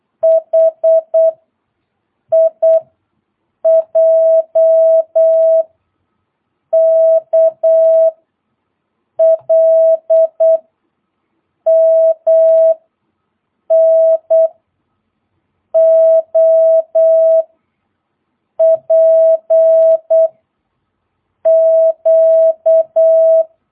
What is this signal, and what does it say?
Morse code, heard as a keyed tone, about 8 wpm, 650 Hz. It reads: H I J K L M N O P Q
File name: jh8cbh / 2022cw